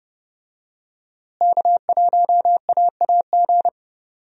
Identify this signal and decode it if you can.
Morse code, heard as a keyed tone, about 30 wpm, 700 Hz. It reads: K1AAG
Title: USA Call Signs - 5 Characters - Repeated 3x - 30wpm